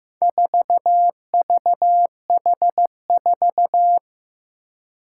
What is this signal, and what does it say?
Morse code, heard as a keyed tone, about 15 wpm, 700 Hz. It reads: 4VH4
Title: Four Challenging Letter-Number Combinations 15wpm